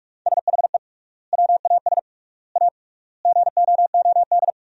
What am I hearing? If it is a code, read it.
Morse code, heard as a keyed tone, about 41 wpm, 700 Hz. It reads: SHE WAS A GOOD
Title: 3x Word-Spacing - Sentences from Top 400 Words - 45wpm